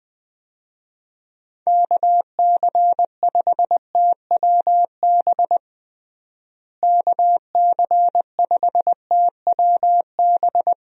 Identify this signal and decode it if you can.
Morse code, heard as a keyed tone, about 20 wpm, 700 Hz. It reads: KC5TWB KC5TWB
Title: USA Call Signs - 6 Characters - Repeated 2x - 20wpm